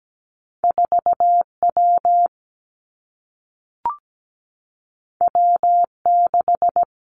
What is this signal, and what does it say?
Morse code, heard as a keyed tone, about 17 wpm, 700 Hz. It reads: 4W W6
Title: Two Challenging Letter-Number Combinations 17wpm